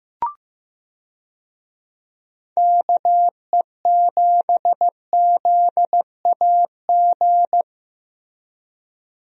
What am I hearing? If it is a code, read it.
Morse code, heard as a keyed tone, about 15 wpm, 700 Hz. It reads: KE7ZAG
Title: USA Call Signs - 6 Characters - Repeated 3x - 15wpm